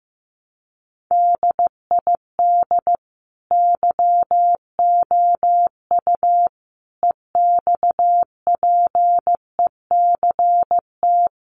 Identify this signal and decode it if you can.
Morse code, heard as a keyed tone, about 14 wpm, 700 Hz. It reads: DID YOU EXPECT
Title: Sentences from Top 1100 Words 15wpm